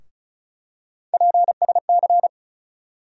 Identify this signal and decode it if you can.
Morse code, heard as a keyed tone, about 36 wpm, 700 Hz. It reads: PSC
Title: USA Call Sign Postfixes - 3 Letters 35wpm